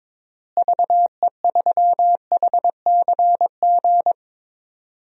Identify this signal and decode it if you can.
Morse code, heard as a keyed tone, about 21 wpm, 700 Hz. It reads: VE3HCG
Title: DXCC-100 Call Signs - 6 Characters - Repeated 3x - 22wpm